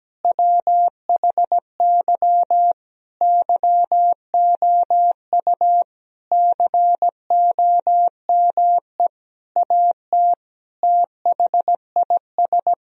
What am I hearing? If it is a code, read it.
Morse code, heard as a keyed tone, about 17 wpm, 700 Hz. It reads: WHY YOU COME AT THIS
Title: Sentences from Top 1000 Words - Review Part 2 - 17wpm